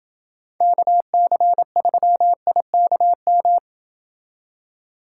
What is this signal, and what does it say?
Morse code, heard as a keyed tone, about 28 wpm, 700 Hz. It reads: KC3IKM